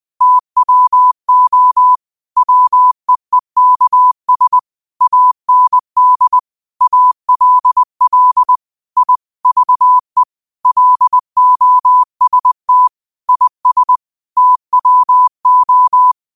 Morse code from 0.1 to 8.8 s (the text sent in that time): TWO WEEKS AND ALL